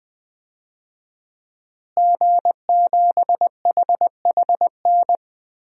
Morse code, heard as a keyed tone, about 20 wpm, 700 Hz. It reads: G7HHN
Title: DXCC-100 Call Signs - 5 Characters - Repeated 2x - 20wpm